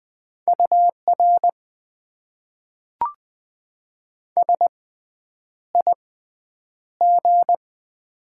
Morse code, heard as a keyed tone, about 20 wpm, spaced as for 9 wpm, 700 Hz. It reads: UR SIG
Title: Top 90 QSO Elements - 6x Character Spacing - 20wpm